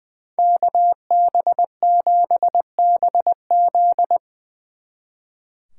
Morse code, heard as a keyed tone, about 23 wpm, 700 Hz. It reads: KB7BZ
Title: Speed Racing - USA 2x2 Call Signs 20wpm